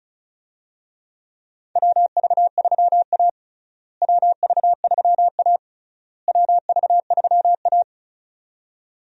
Morse code, heard as a keyed tone, about 36 wpm, 700 Hz. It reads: WV3A WV3A WV3A
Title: USA Call Signs - 4 Characters - Repeated 3x - 35wpm